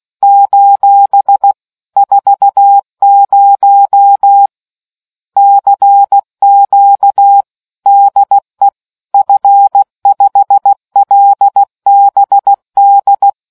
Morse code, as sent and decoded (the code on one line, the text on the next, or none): ---... / ....- ----- / -.-. --.- / -.. / ..-. ..... .-.. -... -..
: 40 CQ DE F5LBD